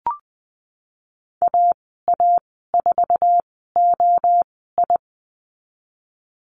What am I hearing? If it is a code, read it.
Morse code, heard as a keyed tone, about 19 wpm, spaced as for 14 wpm, 700 Hz. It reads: AA4OI
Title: 069 Lesson 12g - Letter d - Call Signs 20wpm